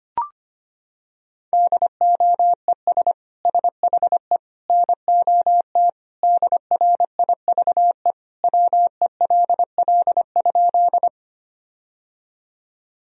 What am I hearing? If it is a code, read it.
Morse code, heard as a keyed tone, about 25 wpm, 700 Hz. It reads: DOES SHE NOT DRIVE WELL?
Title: Sentences from Top 1000 Words - Review Part 2 - 25wpm